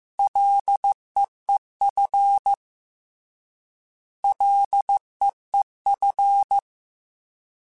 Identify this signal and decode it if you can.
Morse code, heard as a keyed tone, about 15 wpm, 800 Hz. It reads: LEEF LEEF